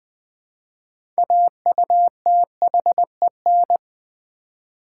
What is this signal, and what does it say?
Morse code, heard as a keyed tone, about 20 wpm, 700 Hz. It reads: AUTHEN
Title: Build A Word - 3 Letters At A Time - 9-Letter Words - 20wpm